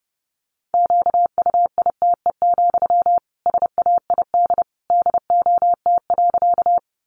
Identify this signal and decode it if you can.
Morse code, heard as a keyed tone, about 30 wpm, 700 Hz. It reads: QUITE, SAID DOT.
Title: Sentences from Top 1500 Words 30wpm